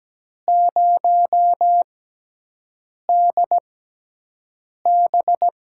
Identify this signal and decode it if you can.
Morse code, heard as a keyed tone, about 17 wpm, spaced as for 6 wpm, 700 Hz. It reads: 0DB
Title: USA Call Signs - 4 Characters - 6x Character Spacing - 17wpm